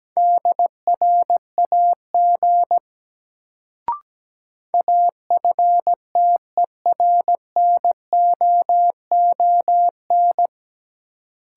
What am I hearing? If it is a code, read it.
Morse code, heard as a keyed tone, about 17 wpm, 700 Hz. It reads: DRAG AFTERNOON
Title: Sentences from Top 1500 Words - New Word Drill Alt - 17wpm